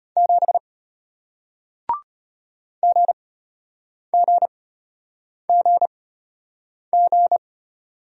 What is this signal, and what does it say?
Morse code, heard as a keyed tone, about 33 wpm, spaced as for 8 wpm, 700 Hz. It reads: 7 GGGG